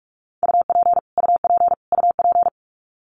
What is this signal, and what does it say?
Morse code, heard as a keyed tone, about 46 wpm, 700 Hz. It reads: UP UP UP
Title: Top 300 Words Repeated 45wpm